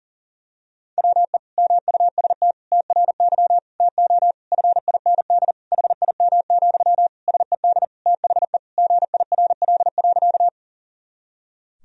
Morse code, heard as a keyed tone, about 37 wpm, 700 Hz. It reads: WE MUST TRY TO FIND HIM, SED THE GIRL.
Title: Sentences from Top 400 Words and Abbreviations 40wpm